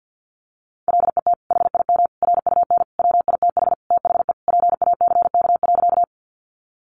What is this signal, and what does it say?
Morse code, heard as a keyed tone, about 50 wpm, 700 Hz. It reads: LET HIM RUN WITH THE PACK.
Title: Sentences from Top 1300 Words 50wpm